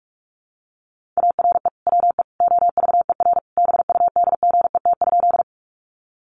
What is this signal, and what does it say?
Morse code, heard as a keyed tone, about 44 wpm, 700 Hz. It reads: ARE WE OVER BUDGET?
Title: Sentences from Top 1200 Words 45wpm